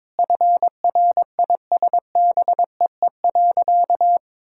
Morse code, heard as a keyed tone, about 23 wpm, 690 Hz. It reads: FRISBEE.
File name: practice / 22wpm / Easy Sentences with Hints - 22wpm